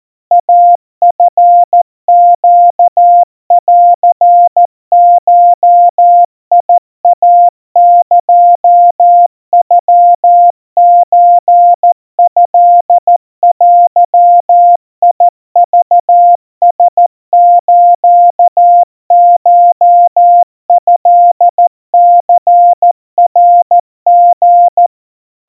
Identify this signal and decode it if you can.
Morse code (Japanese Wabun code), heard as a keyed tone, about 14 wpm, 700 Hz. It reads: イチネンゴイエノソトデクラスコトニナリ
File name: kawa / Morse sample / kamyu 50 cpm 700 Hz Crispy